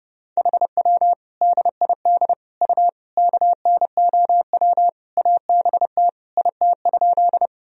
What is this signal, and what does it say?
Morse code, heard as a keyed tone, about 30 wpm, 700 Hz. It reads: HW DID U KNOW ABT IT?
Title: Sentences from Top 100 Words and Abbreviations 30wpm